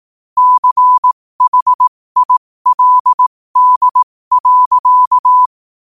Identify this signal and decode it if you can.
Morse code, heard as a keyed tone, about 18 wpm, spaced as for 15 wpm, 1000 Hz. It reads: CHILD.
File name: qotdfiles / 15wpm